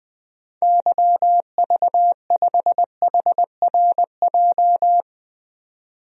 Morse code, heard as a keyed tone, about 21 wpm, 700 Hz. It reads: YV5HRJ
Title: DXCC-100 Call Signs - 6 Characters - 20wpm